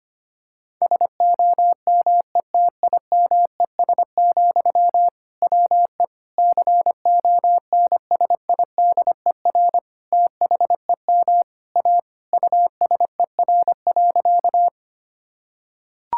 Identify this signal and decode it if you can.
Morse code, heard as a keyed tone, about 25 wpm, 700 Hz. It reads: SOMETIMES, WE CONSIDER THEM A USER.
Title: Sentences from Top 1000 Words - Review Part 5 - 25wpm